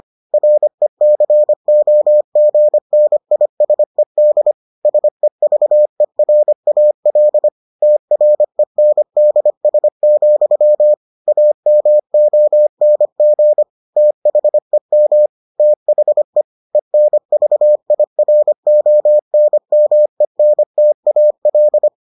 Morse code, heard as a keyed tone, about 25 wpm, 600 Hz. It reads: RECOGNISED SEVERAL TRENDS, AMONG THEM THE ENVIRONMENTAL